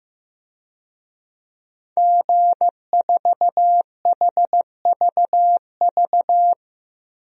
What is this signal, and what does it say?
Morse code, heard as a keyed tone, about 15 wpm, 700 Hz. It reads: G4HVV
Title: DXCC-100 Call Signs - 5 Characters - Repeated 3x - 15wpm